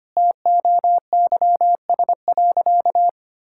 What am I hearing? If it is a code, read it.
Morse code, heard as a keyed tone, about 25 wpm, 700 Hz. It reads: TOYS.
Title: Pangram Challenge v2 - 25wpm